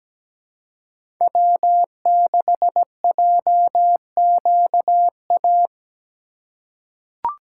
Morse code, heard as a keyed tone, about 17 wpm, 700 Hz. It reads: W6JQA
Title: USA Call Signs - 5 Characters - 2x Character Spacing - 17wpm